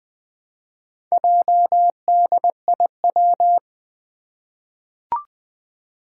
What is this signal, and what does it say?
Morse code, heard as a keyed tone, about 21 wpm, 700 Hz. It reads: JDIW
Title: Four Challenging Letter-Number Combinations 20wpm